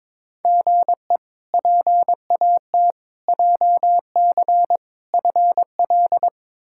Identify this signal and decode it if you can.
Morse code, heard as a keyed tone, about 22 wpm, 700 Hz. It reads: GE PAT JC FL
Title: K1USN SST QSO Practice - 22wpm